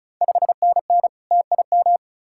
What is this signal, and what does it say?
Morse code, heard as a keyed tone, about 36 wpm, 690 Hz. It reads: 5NN TIM
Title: Ontario QSO Party Exchange 35wpm